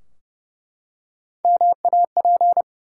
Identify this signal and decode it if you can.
Morse code, heard as a keyed tone, about 28 wpm, 700 Hz. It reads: MAP
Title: Three Letter Words - 30wpm